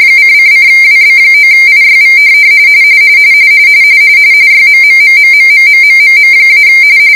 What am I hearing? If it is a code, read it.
RTTY, about 45.45 baud. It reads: :::WB9IYV YMMMMMMMMMM-